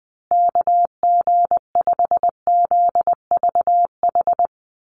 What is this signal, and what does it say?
Morse code, heard as a keyed tone, about 20 wpm, 700 Hz. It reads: KG5ZVH
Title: USA Call Signs Repeated 2x 20wpm